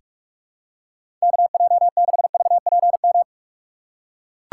Morse code, heard as a keyed tone, about 42 wpm, 700 Hz. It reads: KJ6VPM
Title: USA Call Signs - 6 Characters - 6x Character Spacing - 45wpm